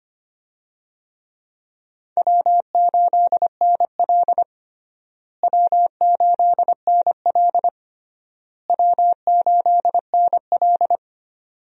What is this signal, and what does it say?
Morse code, heard as a keyed tone, about 25 wpm, 700 Hz. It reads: W8NL W8NL W8NL
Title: USA Call Signs - 4 Characters - Repeated 3x - 25wpm